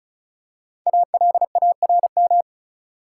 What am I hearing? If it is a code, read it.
Morse code, heard as a keyed tone, about 35 wpm, 700 Hz. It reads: ALARM
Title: Five Letter Words - 35wpm